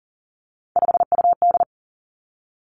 Morse code, heard as a keyed tone, about 40 wpm, 700 Hz. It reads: 5UD
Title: Three Challenging Letter-Number Combinations 40wpm